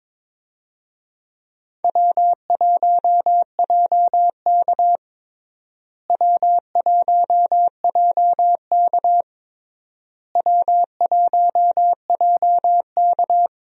Morse code, heard as a keyed tone, about 22 wpm, 700 Hz. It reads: W1JK W1JK W1JK
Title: USA Call Signs - 4 Characters - Repeated 3x - 22wpm